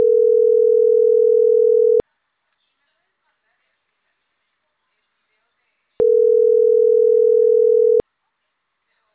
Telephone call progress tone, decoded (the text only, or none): ringback tone